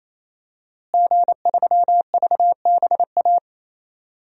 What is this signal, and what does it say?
Morse code, heard as a keyed tone, about 28 wpm, 700 Hz. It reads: G3VBA
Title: DXCC-100 Call Signs - 5 Characters - Repeated 3x - 28wpm